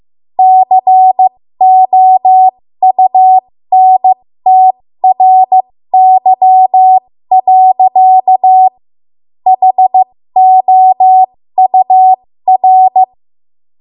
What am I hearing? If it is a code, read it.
Morse code, heard as a keyed tone, about 15 wpm, spaced as for 13 wpm, 750 Hz. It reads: COUNTRY. HOUR
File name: Morse / 13WPM